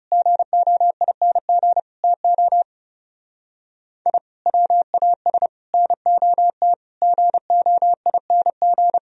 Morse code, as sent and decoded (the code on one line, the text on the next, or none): --. --- .. -. --. / - --- / .. / .-- .- ... / -. --- - / --. --- .. -. --.
GOING TO I WAS NOT GOING